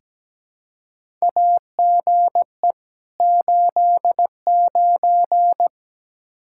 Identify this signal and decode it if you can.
Morse code, heard as a keyed tone, about 17 wpm, 700 Hz. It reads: AGE 89
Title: Practice QSO - Exchange 3 - More Info - 17wpm